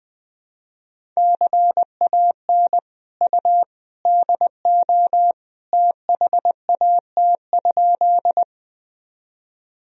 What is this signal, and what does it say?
Morse code, heard as a keyed tone, about 21 wpm, 700 Hz. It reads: CAN U DO THAT?
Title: Sentences from Top 100 Words and Abbreviations 20wpm